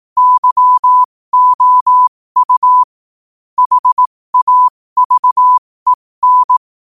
Morse code, heard as a keyed tone, about 18 wpm, spaced as for 15 wpm, 1000 Hz. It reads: YOU HAVEN
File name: 15wpm